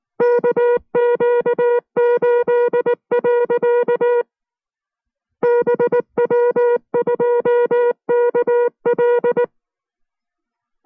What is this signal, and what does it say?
Morse code, heard as a keyed tone, about 19 wpm, 480 Hz. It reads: KQ8. BW2KL